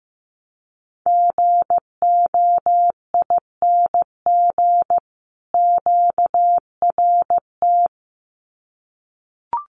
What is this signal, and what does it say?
Morse code, heard as a keyed tone, about 15 wpm, 700 Hz. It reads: GOING QRT
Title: Common QSO Phrases 15wpm